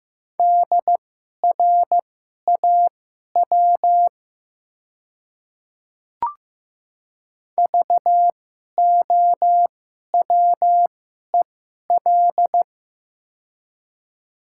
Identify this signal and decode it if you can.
Morse code, heard as a keyed tone, about 15 wpm, spaced as for 11 wpm, 700 Hz. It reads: DRAW VOWEL
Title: 2x Character Spacing - Top 500 Words - 15wpm